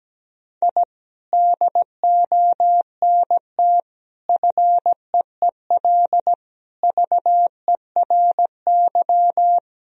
Morse code, heard as a keyed tone, about 17 wpm, 700 Hz. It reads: I DONT FEEL VERY
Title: Sentences from Top 1000 Words - Review Part 3 - 17wpm